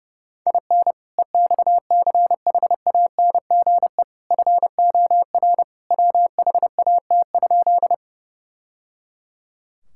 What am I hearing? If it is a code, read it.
Morse code, heard as a keyed tone, about 30 wpm, 700 Hz. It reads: IN EXCHANGE FOR WHAT?
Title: Sentences from Top 1000 Words - Review Part 6 - 30wpm